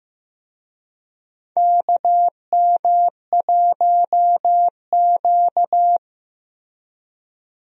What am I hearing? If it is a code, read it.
Morse code, heard as a keyed tone, about 15 wpm, 700 Hz. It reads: KM1Q
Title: USA Call Signs - 4 Characters - Repeated 3x - 15wpm